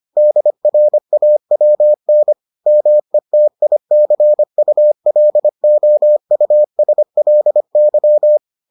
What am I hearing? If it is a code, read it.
Morse code, heard as a keyed tone, about 25 wpm, 600 Hz. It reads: DRAWN METICULOUSLY